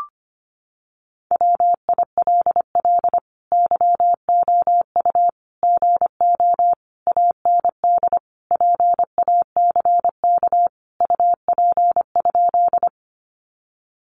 Morse code, heard as a keyed tone, about 25 wpm, 700 Hz. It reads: WILL YOU GO AND PACK UP?
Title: Sentences from Top 1300 Words 25wpm